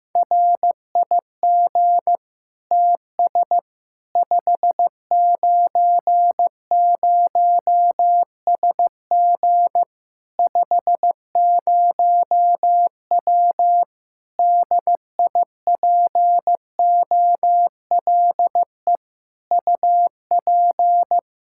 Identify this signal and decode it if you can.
Morse code, heard as a keyed tone, about 15 wpm, 700 Hz. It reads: RIG TS 590SG 50W DIPOLE UP
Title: Practice QSO - Exchange 2 - Info - 15wpm